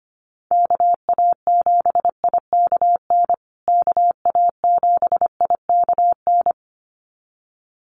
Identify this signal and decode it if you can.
Morse code, heard as a keyed tone, about 25 wpm, 700 Hz. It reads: KA7IKN KA7IKN
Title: USA Call Signs Repeated 2x 25wpm